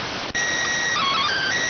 SSTV (slow-scan television), PD 290.